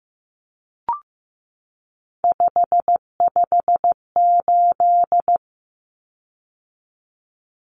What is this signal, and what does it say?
Morse code, heard as a keyed tone, about 15 wpm, 700 Hz. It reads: 558
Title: Common Cut Numbers - 3-Digit 15wpm